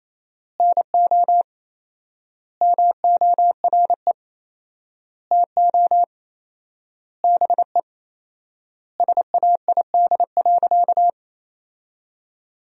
Morse code, heard as a keyed tone, about 28 wpm, 700 Hz. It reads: NO MORE TO BE SAID.
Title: 4x Word-Spacing - Sentences from Top 100 Words - 28wpm